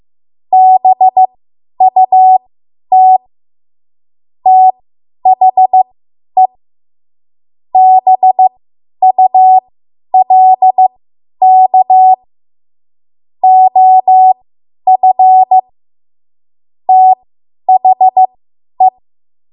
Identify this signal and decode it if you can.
Morse code, heard as a keyed tone, about 15 wpm, spaced as for 10 wpm, 750 Hz. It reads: BUT THE BULK OF THE
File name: Archive / 10 WPM / 10WPM